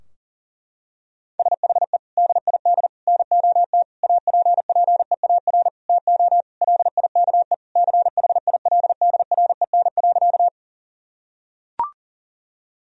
Morse code, heard as a keyed tone, about 40 wpm, 700 Hz. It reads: SHE DID NOT APPEAR TO LIKE CHILDREN.
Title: Sentences from Top 1000 Words - Review Part 2 - 40wpm